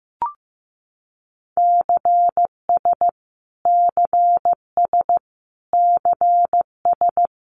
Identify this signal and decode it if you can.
Morse code, heard as a keyed tone, about 15 wpm, 700 Hz. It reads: CS CS CS
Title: Abbreviations Repeated 3x - 15wpm